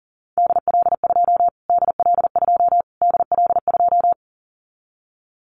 Morse code, heard as a keyed tone, about 40 wpm, 700 Hz. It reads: DL2 DL2 DL2
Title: Callsign Prefixes for Top 25 DXCC - Repeated 3x - 40wpm